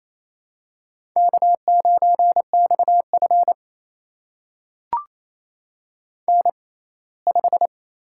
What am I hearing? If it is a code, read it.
Morse code, heard as a keyed tone, about 28 wpm, spaced as for 19 wpm, 700 Hz. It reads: K9XF N5
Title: USA Call Signs - 4 Characters - 6x Character Spacing - 28wpm